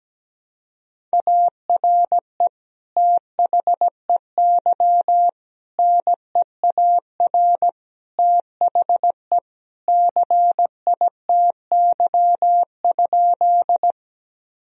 Morse code, heard as a keyed tone, about 17 wpm, 700 Hz. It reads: ARE THEY NEAR THE CITY?
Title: Sentences from Top 1000 Words - Review Part 1 - 17wpm